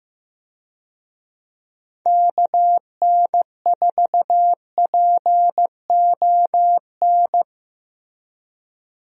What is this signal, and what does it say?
Morse code, heard as a keyed tone, about 15 wpm, 700 Hz. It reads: KN4PON
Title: USA Call Signs - 6 Characters - Repeated 3x - 15wpm